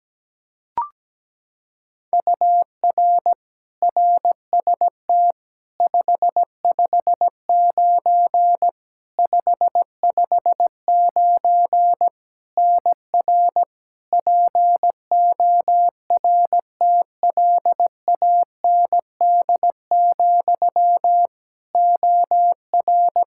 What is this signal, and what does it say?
Morse code, heard as a keyed tone, about 17 wpm, 700 Hz. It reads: UR RST 559 559 NR PORTLAND, OR
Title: Practice QSO - Exchange 1 - Beginning - 17wpm